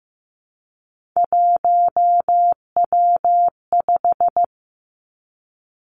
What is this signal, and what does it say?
Morse code, heard as a keyed tone, about 15 wpm, 700 Hz. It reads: 1W5
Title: Three Challenging Letter-Number Combinations 15wpm